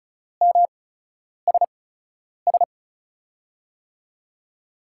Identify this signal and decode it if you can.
Morse code, heard as a keyed tone, about 34 wpm, spaced as for 10 wpm, 700 Hz. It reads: MSS
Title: USA Call Signs - 6 Characters - 8x Character Spacing - 35wpm